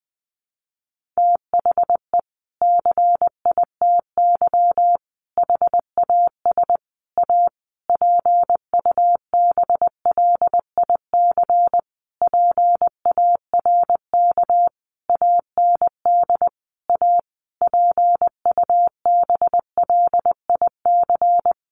Morse code, heard as a keyed tone, about 19 wpm, 700 Hz. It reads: THE CITY HAS A PUBLIC PARK AND A PUBLIC
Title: Sentences from Top 1300 Words 20wpm